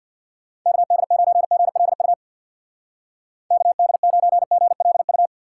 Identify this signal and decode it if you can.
Morse code, heard as a keyed tone, about 63 wpm, 700 Hz. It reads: KD9GLU KD9GLU